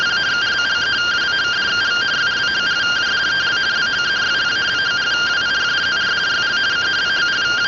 Two-tone teleprinter signal, about 170 Hz shift.